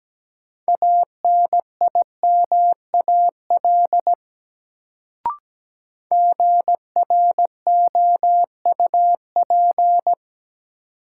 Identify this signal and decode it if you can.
Morse code, heard as a keyed tone, about 17 wpm, 700 Hz. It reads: ANIMAL GROUP
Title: Sentences from Top 1000 Words - Review Part 1 - New Word Drill - 17wpm